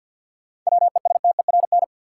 Morse code, heard as a keyed tone, about 53 wpm, 700 Hz. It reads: WESTERN
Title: Sentences from Top 1000 Words - Review Part 7 - New Word Drill - 50wpm